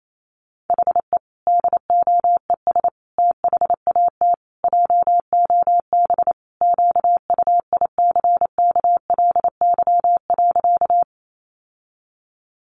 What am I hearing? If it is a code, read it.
Morse code, heard as a keyed tone, about 28 wpm, 700 Hz. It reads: HE DOES THAT JOB QUICKLY.
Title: Sentences from Top 700 Words 28wpm